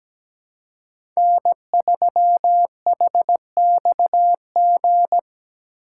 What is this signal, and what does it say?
Morse code, heard as a keyed tone, about 17 wpm, 700 Hz. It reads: N3HXG
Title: USA Call Signs - 5 Characters - 2x Character Spacing - 17wpm